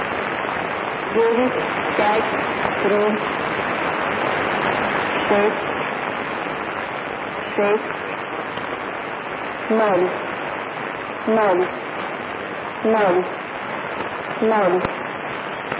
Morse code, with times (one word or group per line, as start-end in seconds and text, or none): none